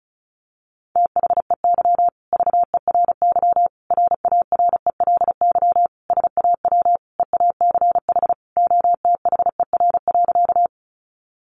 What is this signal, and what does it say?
Morse code, heard as a keyed tone, about 34 wpm, 700 Hz. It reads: THEY VERY RARELY SAW EACH OTHER.
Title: Sentences from Top 1400 Words 35wpm